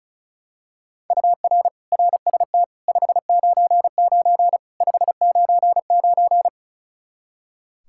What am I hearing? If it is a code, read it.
Morse code, heard as a keyed tone, about 36 wpm, 700 Hz. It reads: UR RST 599 599
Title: Practice QSO - Exchange 1 - Beginning - 35wpm